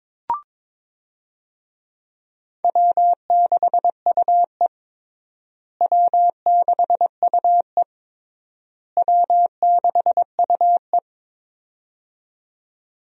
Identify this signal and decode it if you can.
Morse code, heard as a keyed tone, about 21 wpm, 700 Hz. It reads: W6UE W6UE W6UE
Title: USA Call Signs - 4 Characters - Repeated 3x - 22wpm